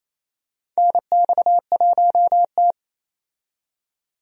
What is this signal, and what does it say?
Morse code, heard as a keyed tone, about 28 wpm, 700 Hz. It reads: NX1T
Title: USA Call Signs - 4 Characters - 28wpm